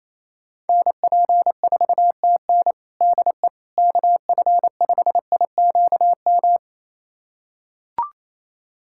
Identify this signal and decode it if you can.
Morse code, heard as a keyed tone, about 28 wpm, 700 Hz. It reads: NP4TN DE KF5IQM
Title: USA Callsign DE Callsign 28wpm